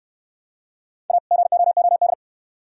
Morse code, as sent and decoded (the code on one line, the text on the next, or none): .- / --. --- --- -..
A GOOD